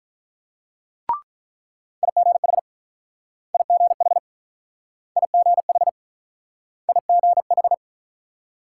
Morse code, heard as a keyed tone, about 48 wpm, 700 Hz. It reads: IGH IGH IGH IGH